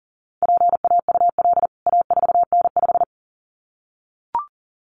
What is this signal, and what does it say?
Morse code, heard as a keyed tone, about 40 wpm, 700 Hz. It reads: PAUL A4N5
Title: CWT Exchange 40wpm